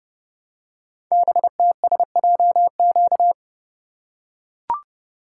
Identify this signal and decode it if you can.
Morse code, heard as a keyed tone, about 30 wpm, 700 Hz. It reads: BTSJQ
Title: Code Groups - 5 Letters - 30wpm